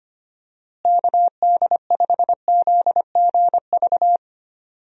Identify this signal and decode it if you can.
Morse code, heard as a keyed tone, about 25 wpm, 700 Hz. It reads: KD5ZGV
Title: USA Call Signs - 6 Characters - Repeated 2x - 25wpm